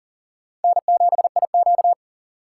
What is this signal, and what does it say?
Morse code, heard as a keyed tone, about 40 wpm, 700 Hz. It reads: N7IQ
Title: USA Call Signs - 3x Character Spacing - 40wpm